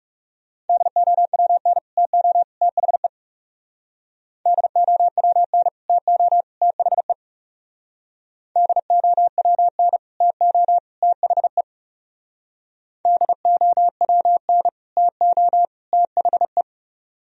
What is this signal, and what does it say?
Morse code, heard as a keyed tone, about 37 wpm, 700 Hz. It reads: DOWN TO THE DOWN TO THE DOWN TO THE DOWN TO THE
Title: Speed-Racing - Sets of 3 Words 30wpm